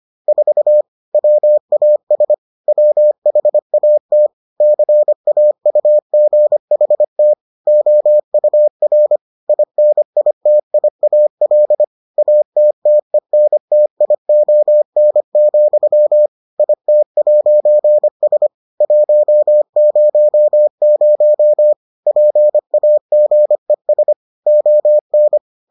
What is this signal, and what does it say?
Morse code, heard as a keyed tone, about 25 wpm, 600 Hz. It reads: 4 WAS WHAT CAUGHT OUR INITIAL ATTENTION, IT'S 100 PAGES ON